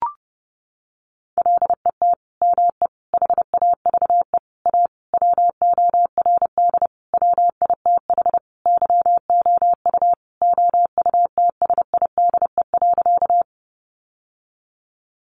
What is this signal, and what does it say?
Morse code, heard as a keyed tone, about 30 wpm, 700 Hz. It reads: LET ME HAVE A WORD WITH YOU OUTSIDE.
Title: Sentences from Top 900 Words 30wpm